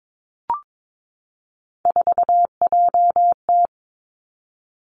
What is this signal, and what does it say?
Morse code, heard as a keyed tone, about 22 wpm, 700 Hz. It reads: 4JT
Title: Three Challenging Letter-Number Combinations 22wpm